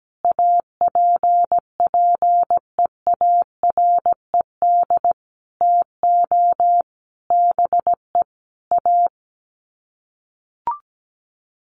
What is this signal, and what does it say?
Morse code, heard as a keyed tone, about 17 wpm, 700 Hz. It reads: APPEARED TO BE A
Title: Sets of 5 Words Encore2 17wpm